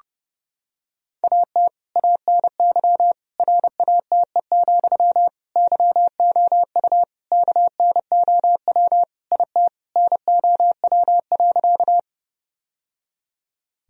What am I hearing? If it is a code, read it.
Morse code, heard as a keyed tone, about 32 wpm, 700 Hz. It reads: AT ANY RATE, YOU KNOW IT NOW.
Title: Sentences from Top 1000 Words - Review Part 4 - 30wpm